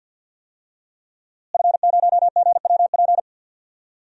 Morse code, heard as a keyed tone, about 46 wpm, 700 Hz. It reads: F0GWP
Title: DXCC-100 Call Signs - 5 Characters - Repeated 2x - 50wpm